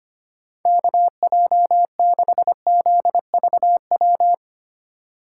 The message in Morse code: KJ6ZVW

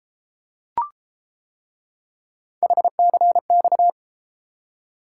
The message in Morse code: HCX

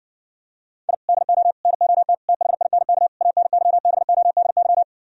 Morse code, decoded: I DO NOT THINK ANYBODY

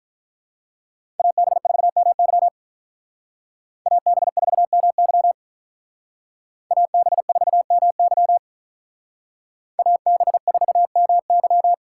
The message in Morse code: AB4MY AB4MY AB4MY AB4MY

